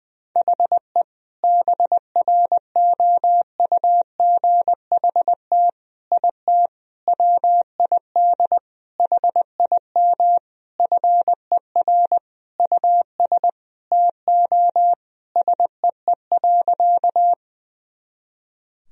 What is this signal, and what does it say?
Morse code, heard as a keyed tone, about 20 wpm, 700 Hz. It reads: HE BROUGHT IT WID HIM FER US TO SEE.
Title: Sentences from Top 500 Words and Abbreviations 20wpm